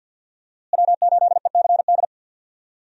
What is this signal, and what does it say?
Morse code, heard as a keyed tone, about 51 wpm, 700 Hz. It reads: W8ECD
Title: USA Call Signs - 5 Characters - Repeated 3x - 50wpm